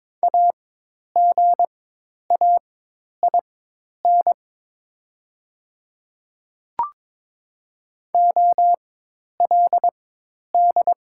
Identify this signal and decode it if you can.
Morse code, heard as a keyed tone, about 22 wpm, spaced as for 10 wpm, 700 Hz. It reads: AGAIN OLD
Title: 4x Character Spacing - Top 200 Words - 22wpm